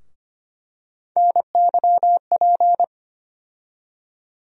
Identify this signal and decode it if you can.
Morse code, heard as a keyed tone, about 25 wpm, 700 Hz. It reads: NYP
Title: USA Call Sign Postfixes - 3 Letters 25wpm